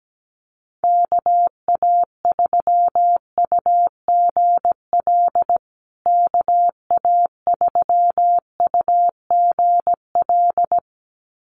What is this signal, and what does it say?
Morse code, heard as a keyed tone, about 17 wpm, 700 Hz. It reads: KA3UGL KA3UGL
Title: USA Call Signs Repeated 2x 17wpm